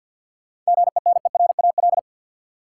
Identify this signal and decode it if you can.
Morse code, heard as a keyed tone, about 53 wpm, 700 Hz. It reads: GENERAL
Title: Sentences from Top 600 Words - New Word Drill - 50wpm